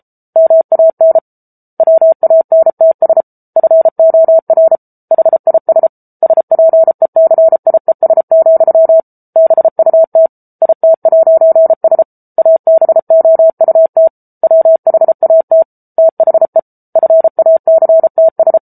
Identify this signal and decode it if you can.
Morse code, heard as a keyed tone, about 34 wpm, 650 Hz. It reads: MAN WANTS FOR HIS SPECIES, BUT IT'S ABOUT WHAT THE FACTS